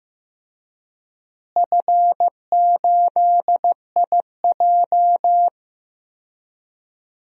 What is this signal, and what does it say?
Morse code, heard as a keyed tone, about 15 wpm, 700 Hz. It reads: F8IJ
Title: DXCC-100 Call Signs - 4 Characters - Repeated 3x - 15wpm